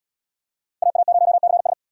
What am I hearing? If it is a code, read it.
Morse code, heard as a keyed tone, about 77 wpm, 700 Hz. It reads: AA0CS